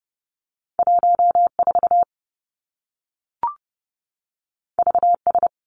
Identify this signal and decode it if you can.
Morse code, heard as a keyed tone, about 30 wpm, 700 Hz. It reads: 14 VS
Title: Two Challenging Letter-Number Combinations 30wpm